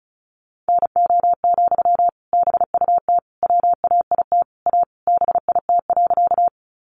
Morse code, heard as a keyed tone, about 34 wpm, 700 Hz. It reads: NO, BUT WAIT A BIT.
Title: Sentences from Top 900 Words 35wpm